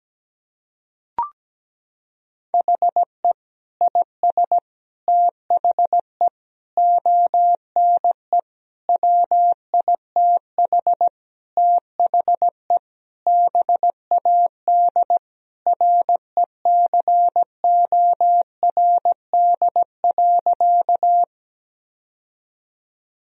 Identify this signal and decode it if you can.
Morse code, heard as a keyed tone, about 17 wpm, 700 Hz. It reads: HE IS THE ONE WITH THE BAD RECORD.
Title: Sentences from Top 1000 Words - Review Part 3 - 17wpm